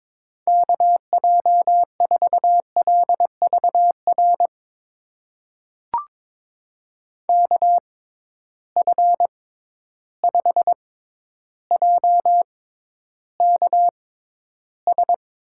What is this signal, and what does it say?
Morse code, heard as a keyed tone, about 22 wpm, spaced as for 11 wpm, 700 Hz. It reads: KJ4LVR KF5JKS
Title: USA Call Signs - 6 Characters - 6x Character Spacing - 22wpm